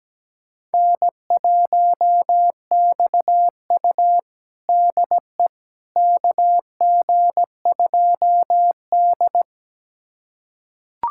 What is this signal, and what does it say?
Morse code, heard as a keyed tone, about 17 wpm, 700 Hz. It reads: N1XU DE KG2D
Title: USA Callsign DE Callsign - 4 Characters - 17wpm